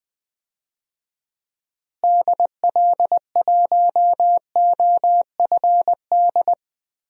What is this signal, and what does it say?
Morse code, heard as a keyed tone, about 20 wpm, 700 Hz. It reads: DL1OFD
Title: DXCC-100 Call Signs - 6 Characters - Repeated 3x - 20wpm